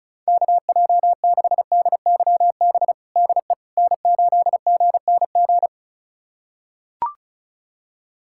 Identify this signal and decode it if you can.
Morse code, heard as a keyed tone, about 36 wpm, 700 Hz. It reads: KJ6DYB DE N8GNG
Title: USA Callsign DE Callsign 35wpm